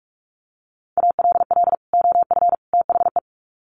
Morse code, heard as a keyed tone, about 44 wpm, 700 Hz. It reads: ALL OF THE
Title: Sets of 3 Words 45wpm